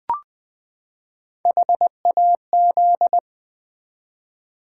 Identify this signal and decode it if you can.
Morse code, heard as a keyed tone, about 20 wpm, 700 Hz. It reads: HAZ
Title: Build A Word - 3 Letters At A Time - 6-Letter Words - 20wpm